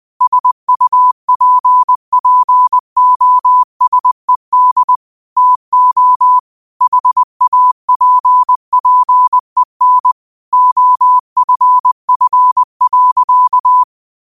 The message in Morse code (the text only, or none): SUPPOSED TO HAPPEN OFF.